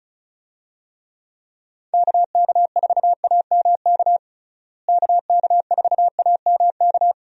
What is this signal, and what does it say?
Morse code, heard as a keyed tone, about 34 wpm, 700 Hz. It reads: KK4AMK KK4AMK
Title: USA Call Signs - 6 Characters - Repeated 3x - 35wpm